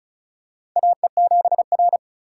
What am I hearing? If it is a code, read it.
Morse code, heard as a keyed tone, about 34 wpm, 690 Hz. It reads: AE7R
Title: USA Call Signs - 4 Characters - 2x Character Spacing - 35wpm